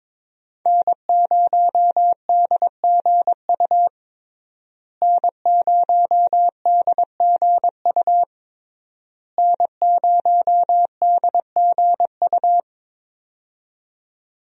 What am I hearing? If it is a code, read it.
Morse code, heard as a keyed tone, about 22 wpm, 700 Hz. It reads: N0DGU N0DGU N0DGU
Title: USA Call Signs - 5 Characters - Repeated 3x - 22wpm